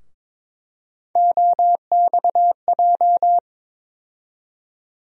Morse code, heard as a keyed tone, about 22 wpm, 700 Hz. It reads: OXJ
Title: USA Call Sign Postfixes - 3 Letters 22wpm